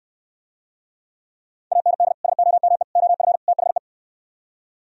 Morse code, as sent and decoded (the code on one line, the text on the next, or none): .- -. -.. / ... --- -- . / --- ..-. / - .... .
AND SOME OF THE